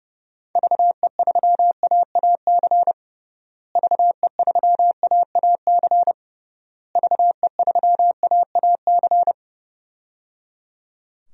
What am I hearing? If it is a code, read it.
Morse code, heard as a keyed tone, about 30 wpm, 700 Hz. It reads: VE3AAC VE3AAC VE3AAC
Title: DXCC-100 Call Signs - 6 Characters - Repeated 3x - 30wpm